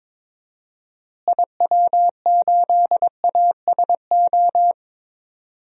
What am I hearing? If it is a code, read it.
Morse code, heard as a keyed tone, about 22 wpm, 700 Hz. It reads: IW8ASO
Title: DXCC-100 Call Signs - 6 Characters - Repeated 2x - 22wpm